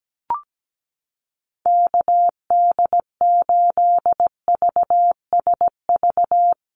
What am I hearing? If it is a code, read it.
Morse code, heard as a keyed tone, about 17 wpm, 700 Hz. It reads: KD8VSV
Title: USA Call Signs Repeated 3x 17wpm